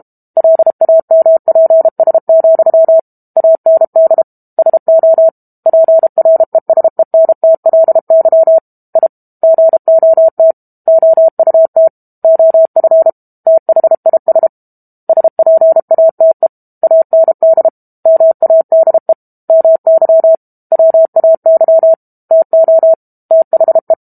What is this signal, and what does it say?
Morse code, heard as a keyed tone, about 32 wpm, 650 Hz. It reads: LAMPS, AND SO PRESENTLY I GOT OUT OF THIS SPATE AND MADE MY WAY TO THE
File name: Book Two / 06-Chapter 6 - 32 WPM